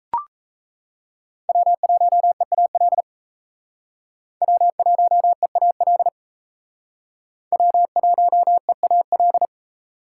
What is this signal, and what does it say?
Morse code, heard as a keyed tone, about 34 wpm, 700 Hz. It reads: W1EAL W1EAL W1EAL
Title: Speed Racing - USA Call Signs 28wpm